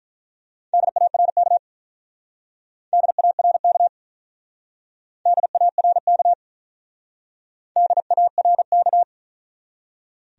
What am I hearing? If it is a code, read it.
Morse code, heard as a keyed tone, about 41 wpm, 700 Hz. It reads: DARK DARK DARK DARK
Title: Speed-Racing Top 500 Words 35wpm